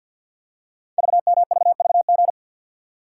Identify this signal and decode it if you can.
Morse code, heard as a keyed tone, about 50 wpm, 690 Hz. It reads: VMVVG
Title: Code Groups - 5 Letters - 50wpm